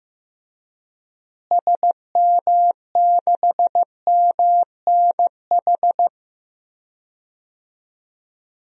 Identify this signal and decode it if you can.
Morse code, heard as a keyed tone, about 15 wpm, 700 Hz. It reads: SM6MNH
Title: DXCC-100 Call Signs - 6 Characters - Repeated 2x - 15wpm